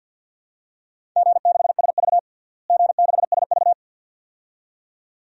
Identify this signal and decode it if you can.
Morse code, heard as a keyed tone, about 51 wpm, 700 Hz. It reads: G6SV G6SV
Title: DXCC-100 Call Signs - 4 Characters - Repeated 2x - 50wpm